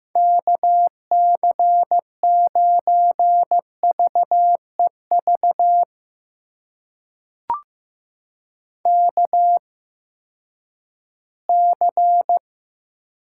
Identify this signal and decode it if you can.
Morse code, heard as a keyed tone, about 15 wpm, spaced as for 10 wpm, 700 Hz. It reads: KC9VEV KC
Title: USA Call Signs - 6 Characters - 8x Character Spacing - 15wpm